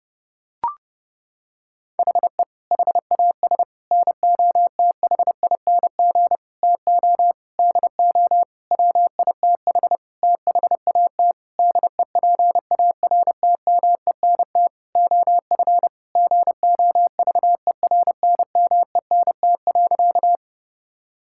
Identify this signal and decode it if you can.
Morse code, heard as a keyed tone, about 32 wpm, 700 Hz. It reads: HE HAS NOTHING TO DO WITH THAT DEPARTMENT OF GOVERNMENT.
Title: Sentences from Top 1000 Words - Review Part 5 - 30wpm